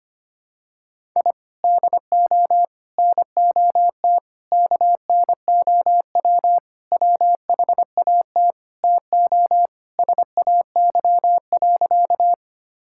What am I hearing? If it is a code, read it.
Morse code, heard as a keyed tone, about 25 wpm, 700 Hz. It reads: I DO NOT KNOW WHAT TO SAY.
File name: practice / 25wpm / Sentences with 4-Word Phrases - 25wpm